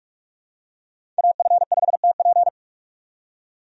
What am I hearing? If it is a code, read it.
Morse code, heard as a keyed tone, about 47 wpm, 700 Hz. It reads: AF5TP
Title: USA Call Signs - 5 Characters - 45wpm